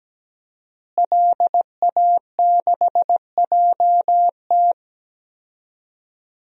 Morse code, heard as a keyed tone, about 17 wpm, 700 Hz. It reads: LA6JT
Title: DXCC-100 Call Signs - 5 Characters - Repeated 3x - 17wpm